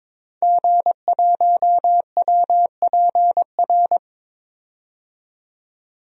Morse code, heard as a keyed tone, about 22 wpm, 700 Hz. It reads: G1WPR
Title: DXCC-100 Call Signs - 5 Characters - 22wpm